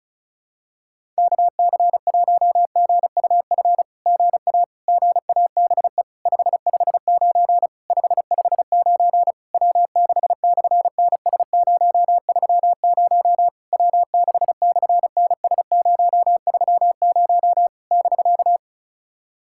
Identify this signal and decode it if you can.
Morse code, heard as a keyed tone, about 33 wpm, 700 Hz. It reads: KC1GUF GA GABE 559 559 W6/NS030 W6/NS030 <BK>
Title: SOTA QSO Practice 35wpm